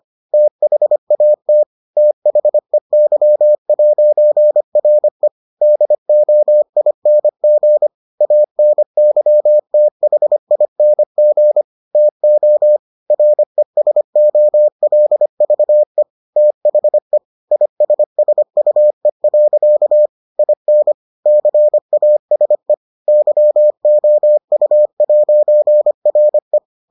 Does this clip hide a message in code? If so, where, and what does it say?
Yes: Morse code from 0.3 to 26.6 s, heard as a keyed tone, about 25 wpm, 600 Hz. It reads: THAT THEY'RE DOING ANYTHING TO RESOLVE THE ISSUE. IN CASE YOU'RE